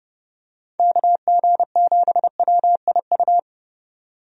Morse code, heard as a keyed tone, about 30 wpm, 700 Hz. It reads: KG7WIU
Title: USA Call Signs - 6 Characters - 30wpm